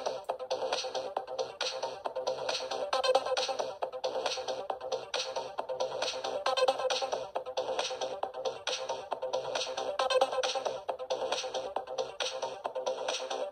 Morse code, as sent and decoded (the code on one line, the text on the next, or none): none